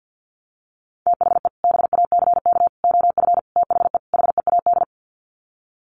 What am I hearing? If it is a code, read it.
Morse code, heard as a keyed tone, about 50 wpm, 700 Hz. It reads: THE BACK OF THE HEAD
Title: Sets of 5 Words Encore2 50wpm